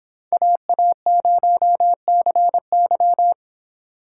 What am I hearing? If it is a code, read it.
Morse code, heard as a keyed tone, about 25 wpm, 700 Hz. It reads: AA0CY